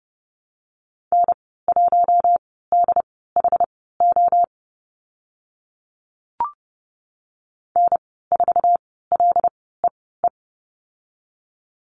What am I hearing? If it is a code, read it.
Morse code, heard as a keyed tone, about 32 wpm, spaced as for 17 wpm, 700 Hz. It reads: N1DHO N4LEE
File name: Lesson 15g - Number 5 - Call Signs 30wpm